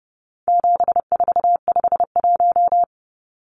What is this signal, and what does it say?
Morse code, heard as a keyed tone, about 30 wpm, 700 Hz. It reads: 7451
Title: Numbers - 4-Digit 30wpm